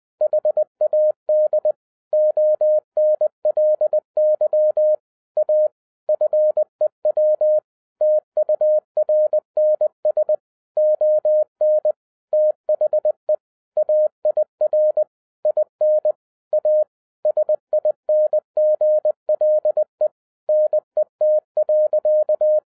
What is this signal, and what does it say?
Morse code, heard as a keyed tone, about 20 wpm, 610 Hz. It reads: HAD ONLY A FEW TURNS ON THE AIR IN A SINGLE NET.
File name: 2012-01-mp3 / n16-THROUGH EIGHT MONTHS-20wpm